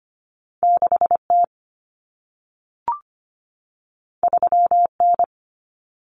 Morse code, heard as a keyed tone, about 25 wpm, 700 Hz. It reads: 6T 3N
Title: Common Cut Numbers - 2-Digit 25wpm